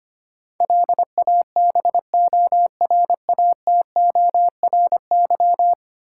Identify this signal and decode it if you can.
Morse code, heard as a keyed tone, about 25 wpm, 700 Hz. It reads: LABORATORY
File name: 10 Letter Words - First 3 Letters Given - 25wpm